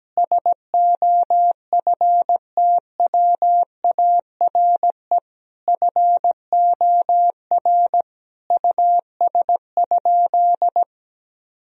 Morse code, heard as a keyed tone, about 17 wpm, 700 Hz. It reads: SOFTWARE FOR US?
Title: Sentences from Top 1000 Words - Review Part 3 - 17wpm